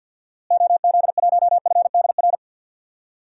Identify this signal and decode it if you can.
Morse code, heard as a keyed tone, about 47 wpm, 700 Hz. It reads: OZ1FDR
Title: DXCC-100 Call Signs - 6 Characters - Repeated 2x - 50wpm